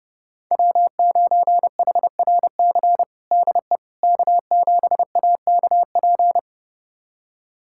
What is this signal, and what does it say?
Morse code, heard as a keyed tone, about 32 wpm, 700 Hz. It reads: W9HRC DE K7AKP
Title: USA Callsign DE Callsign - 5 Characters - 30wpm